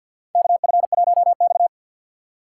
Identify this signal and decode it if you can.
Morse code, heard as a keyed tone, about 52 wpm, 700 Hz. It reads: KF1X
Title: USA Call Signs - 4 Characters - 2x Character Spacing - 50wpm